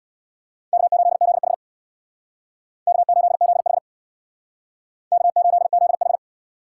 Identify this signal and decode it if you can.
Morse code, heard as a keyed tone, about 64 wpm, 700 Hz. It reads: K8ZH K8ZH K8ZH